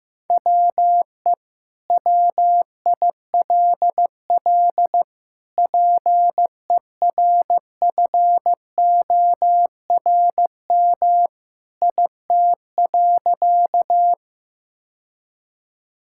Morse code, heard as a keyed tone, about 15 wpm, 700 Hz. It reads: WE WILL PERFORM IT.
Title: Sentences from Top 1000 Words - Review Part 5 - 15wpm